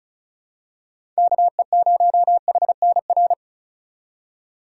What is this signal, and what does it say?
Morse code, heard as a keyed tone, about 36 wpm, 700 Hz. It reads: KE0HNR
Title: USA Call Signs - 6 Characters - 3x Character Spacing - 35wpm